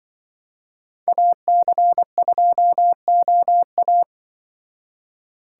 Morse code, heard as a keyed tone, about 25 wpm, 700 Hz. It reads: AC2OA
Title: Speed Racing - USA Call Signs 20wpm